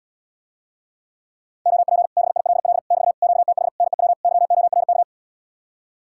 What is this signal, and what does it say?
Morse code, heard as a keyed tone, about 75 wpm, 700 Hz. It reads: OP BILL <BT> QTH NEW YORK